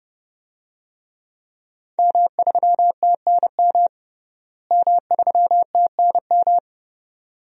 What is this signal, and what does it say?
Morse code, heard as a keyed tone, about 30 wpm, 700 Hz. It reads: M3TNM M3TNM
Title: DXCC-100 Call Signs - 5 Characters - Repeated 2x - 30wpm